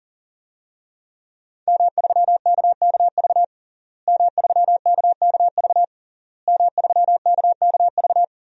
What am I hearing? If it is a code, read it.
Morse code, heard as a keyed tone, about 40 wpm, 700 Hz. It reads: M3KKV M3KKV M3KKV
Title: DXCC-100 Call Signs - 5 Characters - Repeated 3x - 40wpm